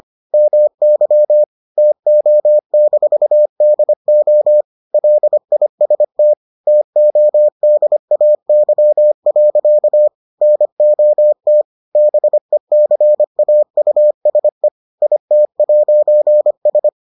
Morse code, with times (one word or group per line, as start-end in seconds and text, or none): 0.3-1.4 MY
1.8-4.6 TO-DO
4.9-6.3 LIST
6.7-10.1 TODAY.
10.4-11.6 NOT
12.0-14.7 BECAUSE
15.0-16.9 IT'S